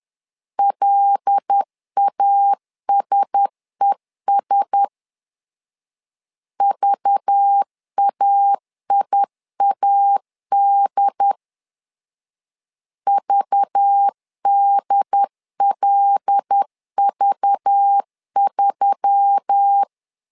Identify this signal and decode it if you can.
Morse code, heard as a keyed tone, about 10 wpm, 800 Hz. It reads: LASES VAIAD VDLV3
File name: LEZIONI / 5 LBVC / 5 LBV 10wpm 2248